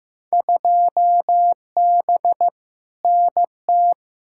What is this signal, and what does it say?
Morse code, heard as a keyed tone, about 15 wpm, 700 Hz. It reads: 2B NT
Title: Field Day Exchanges Encore - 15wpm